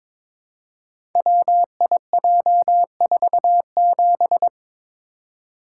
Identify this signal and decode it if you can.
Morse code, heard as a keyed tone, about 22 wpm, 700 Hz. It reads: WIJ47
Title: Code Groups - 5 Letter:Numbers - 22wpm